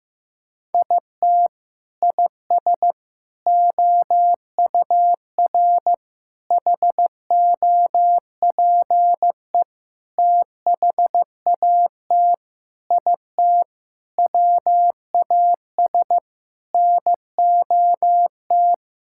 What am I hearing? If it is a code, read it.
Morse code, heard as a keyed tone, about 16 wpm, 700 Hz. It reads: IT IS OUR HOPE THAT IT WAS NOT
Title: Sentences from Top 1000 Words - Review Part 5 - 15wpm